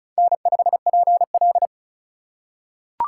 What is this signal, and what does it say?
Morse code, heard as a keyed tone, about 36 wpm, 700 Hz. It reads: N5PL
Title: USA Call Signs - 4 Characters - 4x Character Spacing - 35wpm